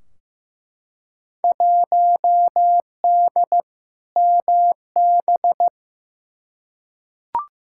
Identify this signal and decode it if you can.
Morse code, heard as a keyed tone, about 15 wpm, 700 Hz. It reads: 1D MB
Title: Field Day Exchanges Encore - 15wpm